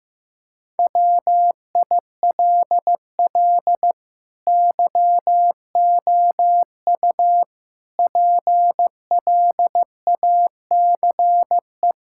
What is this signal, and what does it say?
Morse code, heard as a keyed tone, about 15 wpm, 700 Hz. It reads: WILL YOU PLACE
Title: Sentences from Top 1600 Words 15wpm